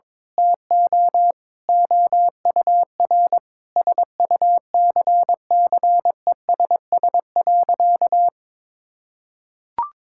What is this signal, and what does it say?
Morse code, heard as a keyed tone, about 22 wpm, 700 Hz. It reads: TO OUR SUCCESS.
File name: Sentences with 9-Word Phrases - 22wpm